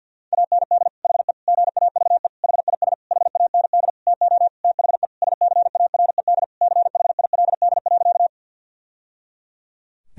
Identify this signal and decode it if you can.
Morse code, heard as a keyed tone, about 50 wpm, 700 Hz. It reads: AND HE GAVE HIS HAND TO THE SCARED CHILD.
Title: Sentences from Top 1000 Words - Review Part 4 - 50wpm